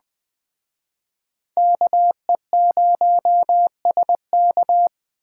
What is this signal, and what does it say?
Morse code, heard as a keyed tone, about 20 wpm, 700 Hz. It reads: KE0SK